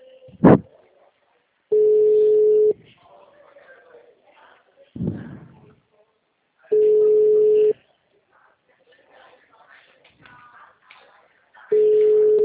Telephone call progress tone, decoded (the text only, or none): ringback tone